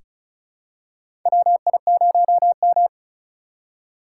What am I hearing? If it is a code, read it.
Morse code, heard as a keyed tone, about 36 wpm, 700 Hz. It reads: WI0M
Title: USA Call Signs - 4 Characters - 4x Character Spacing - 35wpm